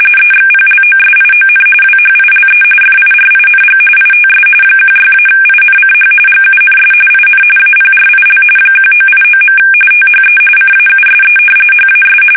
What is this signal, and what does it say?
RTTY, about 45.45 baud. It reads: EXPLOSION WHILE FIRING ITS 16-INCH GUNS IN APRIL 1989. THE CAUSE HAS
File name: fsk850 / battleshipnj